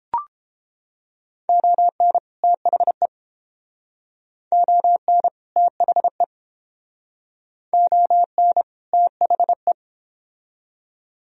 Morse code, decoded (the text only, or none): ON THE ON THE ON THE